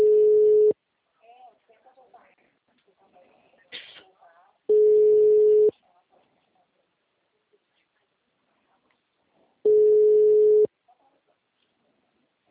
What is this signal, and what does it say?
Ringback tone (Europe, 425 Hz), with speech.